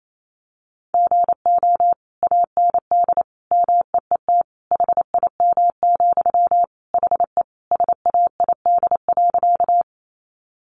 Morse code, heard as a keyed tone, about 28 wpm, 700 Hz. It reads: GO AND MEET HIM, HE SAID.
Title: Sentences from Top 1100 Words 28wpm